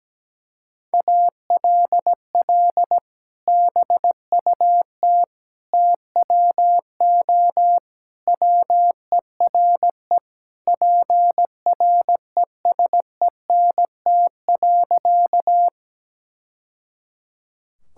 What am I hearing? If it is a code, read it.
Morse code, heard as a keyed tone, about 17 wpm, 700 Hz. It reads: ALL BUT TWO WERE PRESENT.
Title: Sentences from Top 1000 Words - Review Part 5 - 17wpm